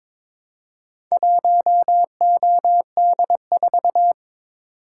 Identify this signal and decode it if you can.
Morse code, heard as a keyed tone, about 22 wpm, 700 Hz. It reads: 1OD4
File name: practice / 22wpm / Four Letter-Number Combinations - 22wpm